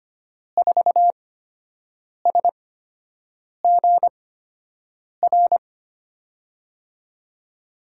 Morse code, heard as a keyed tone, about 25 wpm, spaced as for 7 wpm, 700 Hz. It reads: 4SGR
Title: USA Call Signs - 6 Characters - 8x Character Spacing - 25wpm